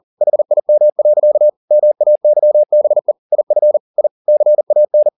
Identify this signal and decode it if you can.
Morse code, heard as a keyed tone, about 40 wpm, 600 Hz. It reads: HIM. MAYBE IF I CAN